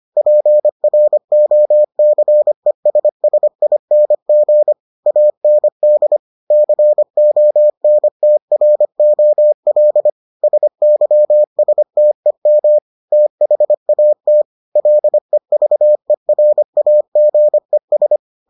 Morse code, heard as a keyed tone, about 25 wpm, 600 Hz. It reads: PROCESSING AND CONTROL SYSTEM THAT LEVERAGES